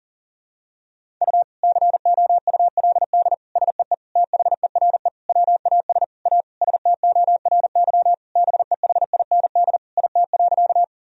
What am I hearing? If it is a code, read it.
Morse code, heard as a keyed tone, about 37 wpm, 700 Hz. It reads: U COULD SEE THERE WAS A STORY BEHIND IT.
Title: Sentences from Top 500 Words and Abbreviations 40wpm